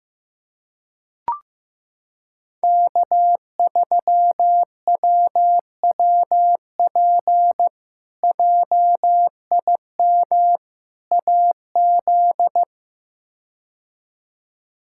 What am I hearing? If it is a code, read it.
Morse code, heard as a keyed tone, about 15 wpm, 700 Hz. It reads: K3WWP JIM AZ
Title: K1USN SST QSO Practice - 15wpm